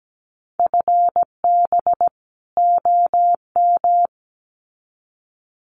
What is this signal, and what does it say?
Morse code, heard as a keyed tone, about 17 wpm, 700 Hz. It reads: FB OM
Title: Common QSO Phrases 17wpm